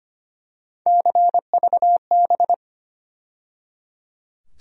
Morse code, heard as a keyed tone, about 25 wpm, 700 Hz. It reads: CVB
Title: USA Call Sign Postfixes - 3 Letters 25wpm